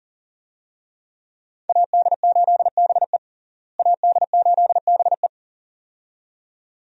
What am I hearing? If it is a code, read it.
Morse code, heard as a keyed tone, about 37 wpm, 700 Hz. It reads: AD8BE AD8BE
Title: USA Call Signs - 5 Characters - Repeated 2x - 40wpm